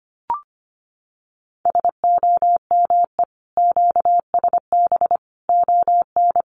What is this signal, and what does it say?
Morse code, heard as a keyed tone, about 25 wpm, 700 Hz. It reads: SOME QSB ON
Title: Common QSO Phrases 25wpm